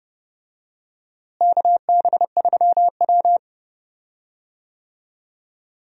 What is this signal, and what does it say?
Morse code, heard as a keyed tone, about 30 wpm, 700 Hz. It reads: KB3W
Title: USA Call Signs - 4 Characters - Repeated 2x - 30wpm